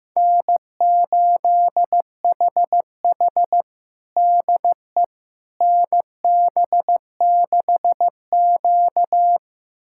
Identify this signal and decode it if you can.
Morse code, heard as a keyed tone, about 16 wpm, 700 Hz. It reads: N8HH DE NB6Q
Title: USA Callsign DE Callsign - 4 Characters - 15wpm